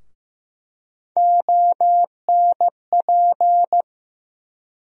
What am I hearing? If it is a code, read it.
Morse code, heard as a keyed tone, about 15 wpm, 700 Hz. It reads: ONP
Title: USA Call Sign Postfixes - 3 Letters 15wpm